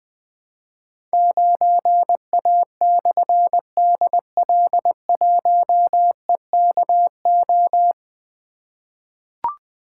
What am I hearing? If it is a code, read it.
Morse code, heard as a keyed tone, about 20 wpm, 700 Hz. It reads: 9A/DL1EKO
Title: Difficult Call Signs 20wpm